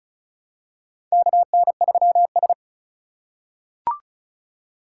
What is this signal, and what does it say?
Morse code, heard as a keyed tone, about 34 wpm, 700 Hz. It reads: KN3S
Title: USA Call Signs - 4 Characters - 6x Character Spacing - 35wpm